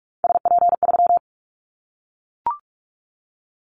Morse code, heard as a keyed tone, about 44 wpm, 700 Hz. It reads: SP3
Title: Callsign Prefixes for Top 25 DXCC - Repeated 3x - 45wpm